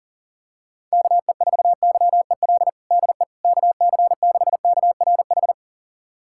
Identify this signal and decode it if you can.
Morse code, heard as a keyed tone, about 40 wpm, 700 Hz. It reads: KE4YEL DE KC6KRH
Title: USA Callsign DE Callsign 40wpm